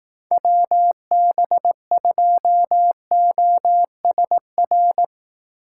Morse code, heard as a keyed tone, about 18 wpm, 700 Hz. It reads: WB2OSR
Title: Speed Racing - USA 2x3 Call Signs 15wpm